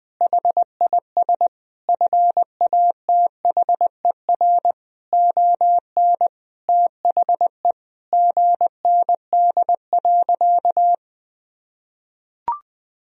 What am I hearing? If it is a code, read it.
Morse code, heard as a keyed tone, about 20 wpm, 700 Hz. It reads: HIS FATHER ON THE GND.
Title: Sentences from Top 500 Words and Abbreviations 20wpm